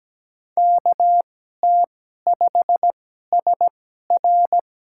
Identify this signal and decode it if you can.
Morse code, heard as a keyed tone, about 17 wpm, spaced as for 12 wpm, 700 Hz. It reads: KT5SR
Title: USA Call Signs - 5 Characters - 2x Character Spacing - 17wpm